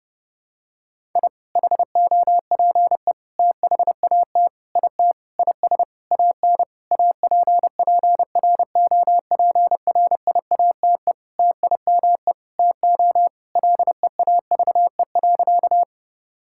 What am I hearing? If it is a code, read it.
Morse code, heard as a keyed tone, about 30 wpm, 700 Hz. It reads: I HOPE THAT IT IS AN APPROPRIATE TIME TO LEAVE.
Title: Sentences from Top 1000 Words - Review Part 5 - 30wpm